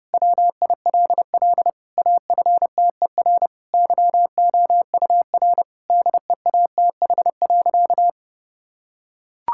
Morse code, heard as a keyed tone, about 30 wpm, 700 Hz. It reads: WILL AFTER YOUR DEATH.
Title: Sentences from Top 1600 Words 30wpm